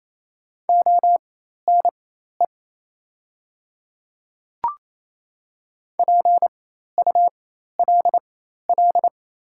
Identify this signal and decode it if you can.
Morse code, heard as a keyed tone, about 28 wpm, spaced as for 13 wpm, 700 Hz. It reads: ONE PULL
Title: 4x Character Spacing - Top 500 Words - 28wpm